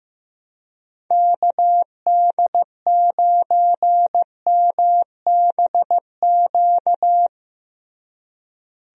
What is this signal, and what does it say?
Morse code, heard as a keyed tone, about 15 wpm, 700 Hz. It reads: KD9MBQ
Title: USA Call Signs - 6 Characters - Repeated 2x - 15wpm